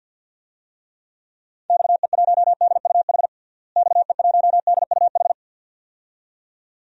Morse code, heard as a keyed tone, about 52 wpm, 700 Hz. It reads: XE1DUH XE1DUH
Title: DXCC-100 Call Signs - 6 Characters - Repeated 2x - 50wpm